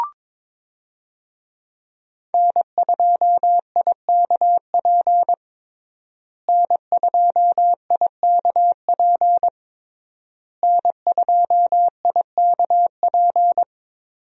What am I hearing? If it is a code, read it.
Morse code, heard as a keyed tone, about 22 wpm, 700 Hz. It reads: N2IKP N2IKP N2IKP
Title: USA Call Signs - 5 Characters - Repeated 3x - 22wpm